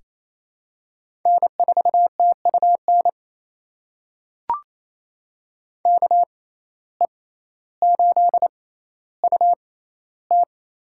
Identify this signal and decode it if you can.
Morse code, heard as a keyed tone, about 28 wpm, spaced as for 14 wpm, 700 Hz. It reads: N4TUN KE8UT
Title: USA Call Signs - 5 Characters - 6x Character Spacing - 28wpm